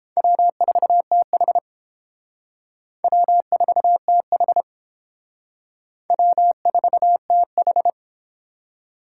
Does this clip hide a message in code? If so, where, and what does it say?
Yes: Morse code from 0.2 to 7.9 s, heard as a keyed tone, about 30 wpm, 700 Hz. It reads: W4TH W4TH W4TH